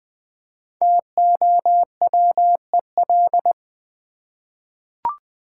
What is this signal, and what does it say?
Morse code, heard as a keyed tone, about 20 wpm, 700 Hz. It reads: TOWEL
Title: Easy Words with Hints - 20wpm